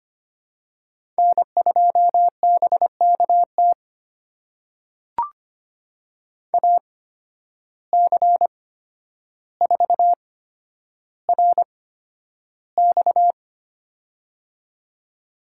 Morse code, heard as a keyed tone, about 25 wpm, spaced as for 11 wpm, 700 Hz. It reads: N2BKT AC4RX
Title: USA Call Signs - 5 Characters - 8x Character Spacing - 25wpm